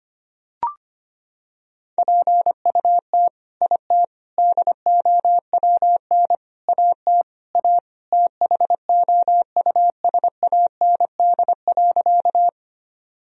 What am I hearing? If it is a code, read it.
Morse code, heard as a keyed tone, about 25 wpm, 700 Hz. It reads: PUT IT DOWN AT A THOUSAND.
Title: Sentences from Top 1000 Words - Review Part 3 - 25wpm